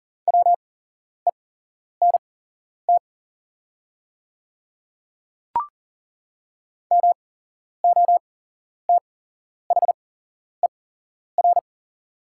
Morse code, heard as a keyed tone, about 40 wpm, spaced as for 11 wpm, 700 Hz. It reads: WENT MOTHER
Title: 8x Character Spacing - Top 200 Words - 40wpm